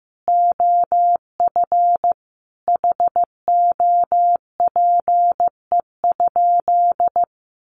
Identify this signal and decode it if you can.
Morse code, heard as a keyed tone, about 14 wpm, 700 Hz. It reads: OF HOPE?
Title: Sentences from Top 1300 Words 15wpm